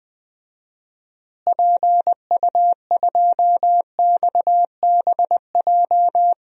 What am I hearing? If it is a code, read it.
Morse code, heard as a keyed tone, about 21 wpm, 700 Hz. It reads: PU2XBJ
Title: DXCC-100 Call Signs - 6 Characters - Repeated 2x - 20wpm